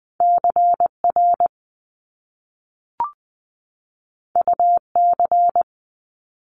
Morse code, heard as a keyed tone, about 20 wpm, 700 Hz. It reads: CR UC
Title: Top Two Letter Combinations in Words 20wpm